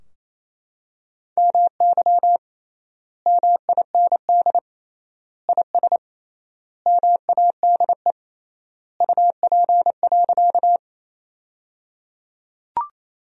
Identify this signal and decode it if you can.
Morse code, heard as a keyed tone, about 28 wpm, 700 Hz. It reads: MY MIND IS MADE UP.
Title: 3x Word-Spacing - Sentences from Top 500 Words - 28wpm